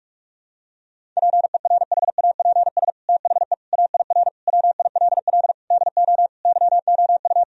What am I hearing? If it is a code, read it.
Morse code, heard as a keyed tone, about 46 wpm, 700 Hz. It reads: PERHAPS THE AIR WILL DO YOU
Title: Sentences from Top 1000 Words - Review Part 1 - 45wpm